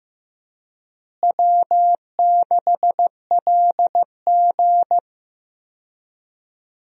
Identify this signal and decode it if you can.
Morse code, heard as a keyed tone, about 15 wpm, 700 Hz. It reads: W6LG
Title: USA Call Signs - 4 Characters - Repeated 3x - 15wpm